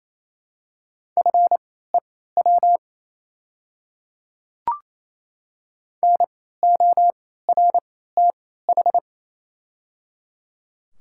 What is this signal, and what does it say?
Morse code, heard as a keyed tone, about 28 wpm, spaced as for 16 wpm, 700 Hz. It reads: FEW NORTH
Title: 3x Character Spacing - Top 300 Words - 28wpm